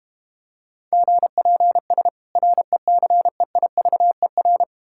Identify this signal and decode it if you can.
Morse code, heard as a keyed tone, about 33 wpm, 700 Hz. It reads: GPS RECEIVER